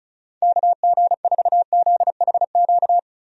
Morse code, heard as a keyed tone, about 34 wpm, 700 Hz. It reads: KG4ZHQ
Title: USA Call Signs - 6 Characters - Repeated 3x - 35wpm